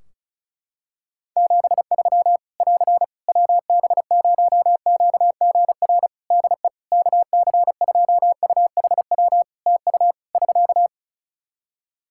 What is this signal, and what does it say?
Morse code, heard as a keyed tone, about 36 wpm, 700 Hz. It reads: 73 <AR> WB0QGR DE KC2UHW TU <SK>
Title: Practice QSO - Exchange 4 - Ending - 35wpm